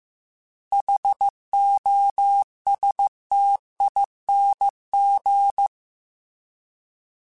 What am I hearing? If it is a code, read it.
Morse code, heard as a keyed tone, about 15 wpm, 800 Hz. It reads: HOSTING